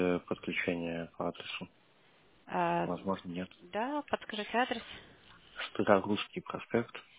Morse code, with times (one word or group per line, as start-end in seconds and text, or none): none